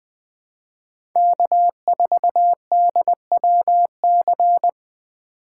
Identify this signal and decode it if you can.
Morse code, heard as a keyed tone, about 20 wpm, 700 Hz. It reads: K4DWC
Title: USA Call Signs - 4x Character Spacing - 20wpm